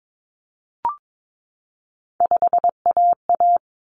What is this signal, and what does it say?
Morse code, heard as a keyed tone, about 22 wpm, 700 Hz. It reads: 5AA
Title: Common Cut Numbers - 3-Digit 22wpm